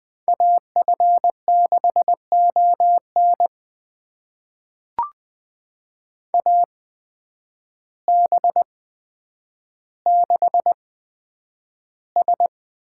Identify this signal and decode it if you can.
Morse code, heard as a keyed tone, about 20 wpm, spaced as for 9 wpm, 700 Hz. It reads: AF6ON AB6S
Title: USA Call Signs - 5 Characters - 8x Character Spacing - 20wpm